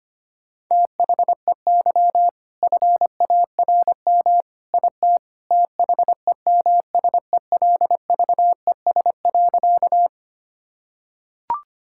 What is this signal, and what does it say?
Morse code, heard as a keyed tone, about 25 wpm, 700 Hz. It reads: THEY FARM IT THEMSELVES.
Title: Sentences from Top 1000 Words - Review Part 5 - 25wpm